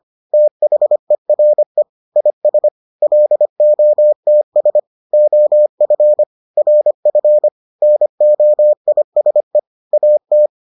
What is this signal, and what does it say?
Morse code, heard as a keyed tone, about 25 wpm, 600 Hz. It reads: THERE IS LOTS OF RF NOISE AT